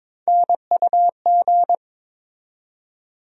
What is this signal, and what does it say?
Morse code, heard as a keyed tone, about 22 wpm, 700 Hz. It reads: NUG
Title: USA Call Sign Postfixes - 3 Letters 22wpm